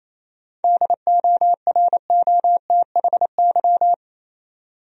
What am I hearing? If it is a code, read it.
Morse code, heard as a keyed tone, about 28 wpm, 700 Hz. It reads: DOROTHY
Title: USA Common Names - 28wpm